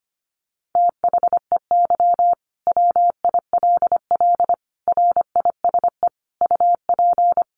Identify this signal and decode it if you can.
Morse code, heard as a keyed tone, about 25 wpm, 700 Hz. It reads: THEY WILL RISE UP
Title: Sentences from Top 900 Words 25wpm